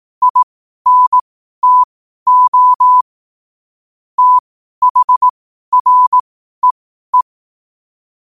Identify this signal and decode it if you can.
Morse code, heard as a keyed tone, about 18 wpm, spaced as for 12 wpm, 1000 Hz. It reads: INTO THREE